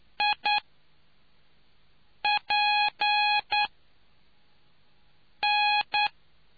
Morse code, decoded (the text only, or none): IPN